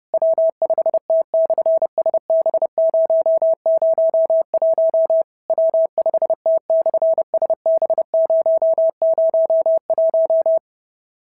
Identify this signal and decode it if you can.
Morse code, heard as a keyed tone, about 30 wpm, 650 Hz. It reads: W5T/SB001 W5T/SB001
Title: SOTA QSO Practice 30wpm